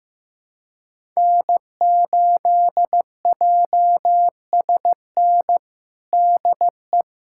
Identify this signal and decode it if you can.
Morse code, heard as a keyed tone, about 15 wpm, 700 Hz. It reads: N8JSN DE
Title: USA Callsign DE Callsign 15wpm